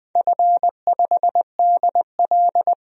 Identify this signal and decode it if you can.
Morse code, heard as a keyed tone, about 20 wpm, 700 Hz. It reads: F5DL
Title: DXCC-100 Call Signs - 4 Characters - 20wpm